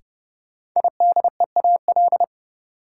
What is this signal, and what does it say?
Morse code, heard as a keyed tone, about 30 wpm, 700 Hz. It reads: IDEAL
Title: Sentences from Top 1400 Words - New Word Drill - 30wpm